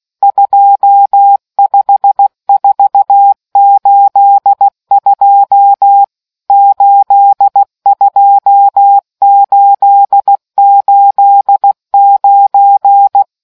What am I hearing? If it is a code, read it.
Morse code, heard as a keyed tone, about 16 wpm, 800 Hz. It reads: 25482 82889